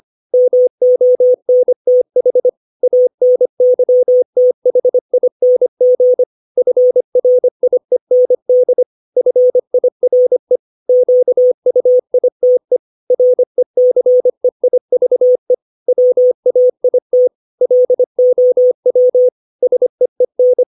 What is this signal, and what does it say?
Morse code, heard as a keyed tone, about 25 wpm, 500 Hz. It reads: MONTH ANYTHING FRIEND FIRE QUITE RECEIVE WAIT LOW SEEN
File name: 632 words / 632 Words 25WPM